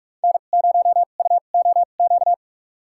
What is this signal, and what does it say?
Morse code, heard as a keyed tone, about 46 wpm, spaced as for 33 wpm, 700 Hz. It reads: N0UOQ
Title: USA Call Signs - 5 Characters - 2x Character Spacing - 45wpm